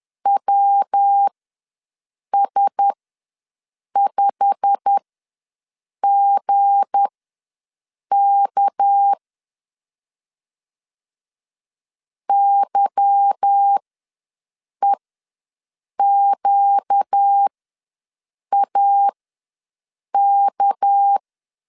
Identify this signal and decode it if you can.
Morse code, heard as a keyed tone, about 10 wpm, spaced as for 6 wpm, 800 Hz. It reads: WS5GK YEQAK
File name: LEZIONI / 8 YJ45 / 8 YJ45 ws 10wpm 3843